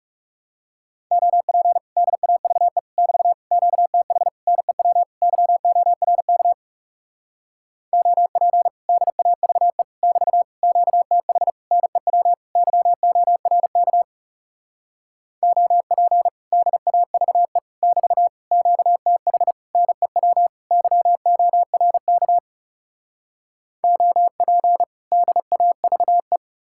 Morse code, heard as a keyed tone, about 40 wpm, 700 Hz. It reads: OP DAVE <BT> QTH NEW YORK OP DAVE <BT> QTH NEW YORK OP DAVE <BT> QTH NEW YORK OP DAVE